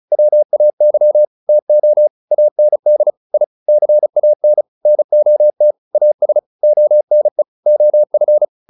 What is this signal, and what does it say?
Morse code, heard as a keyed tone, about 34 wpm, 600 Hz. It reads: WAY TO AND I CAN NOT AS ONE OF